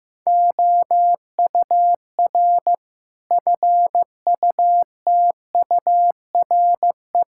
Morse code, heard as a keyed tone, about 15 wpm, 700 Hz. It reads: OUR FUTURE
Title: Sentences from Top 1000 Words - Review Part 5 - 15wpm